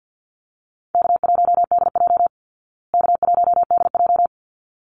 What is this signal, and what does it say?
Morse code, heard as a keyed tone, about 50 wpm, 700 Hz. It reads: K1DJ K1DJ
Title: Easy Call Signs Repeated 2x - 50wpm